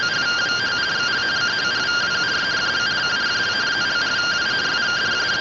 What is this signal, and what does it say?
Two-tone teleprinter signal, about 170 Hz shift.
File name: findings / AXIAL fr1 3mm.gen-38 rtty rx